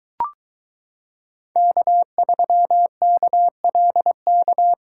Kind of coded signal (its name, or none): Morse code